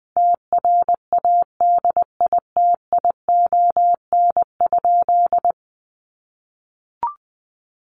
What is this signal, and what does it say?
Morse code, heard as a keyed tone, about 20 wpm, 700 Hz. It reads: TRADITION?
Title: Sentences from Top 1500 Words 20wpm